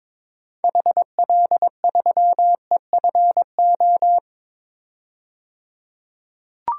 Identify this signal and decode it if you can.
Morse code, heard as a keyed tone, about 22 wpm, 700 Hz. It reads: HL3EFO
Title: DXCC-100 Call Signs - 6 Characters - 22wpm